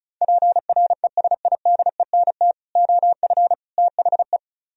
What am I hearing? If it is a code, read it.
Morse code, heard as a keyed tone, about 43 wpm, 700 Hz. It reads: PRESIDENT OF THE
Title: Speed-Racing - Sets of 3 Words 35wpm